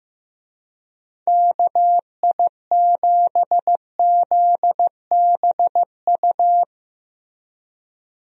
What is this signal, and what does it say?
Morse code, heard as a keyed tone, about 15 wpm, 700 Hz. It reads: KI7ZBU
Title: USA Call Signs - 6 Characters - Repeated 3x - 15wpm